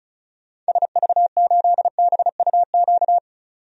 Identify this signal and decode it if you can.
Morse code, heard as a keyed tone, about 36 wpm, 690 Hz. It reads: SV8BUQ
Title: DXCC-100 Call Signs - 6 Characters - 35wpm